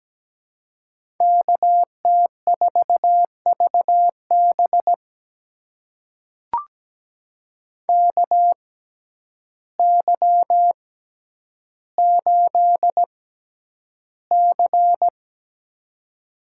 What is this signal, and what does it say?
Morse code, heard as a keyed tone, about 17 wpm, spaced as for 9 wpm, 700 Hz. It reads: KT4VB KY8C
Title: USA Call Signs - 6x Character Spacing - 17wpm